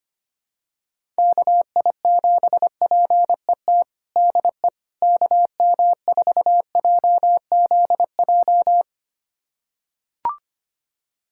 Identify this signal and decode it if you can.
Morse code, heard as a keyed tone, about 25 wpm, 700 Hz. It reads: KI7PET DE KM4JZJ
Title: USA Callsign DE Callsign - 6 Characters - 25wpm